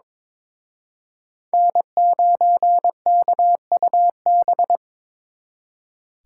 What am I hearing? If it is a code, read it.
Morse code, heard as a keyed tone, about 26 wpm, 700 Hz. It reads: N9KUB
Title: Speed Racing - USA 1x3 Call Signs 22wpm